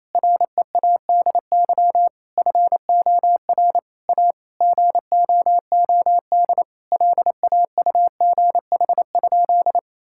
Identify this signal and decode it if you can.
Morse code, heard as a keyed tone, about 27 wpm, 700 Hz. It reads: READY FOR A GOOD LAUGH?
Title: Sentences from Top 1000 Words - Review Part 5 - 28wpm